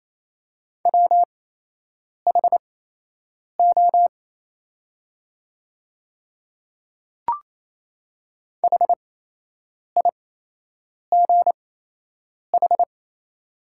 Morse code, heard as a keyed tone, about 27 wpm, spaced as for 8 wpm, 700 Hz. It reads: WHO HIGH
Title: 8x Character Spacing - Top 300 Words - 28wpm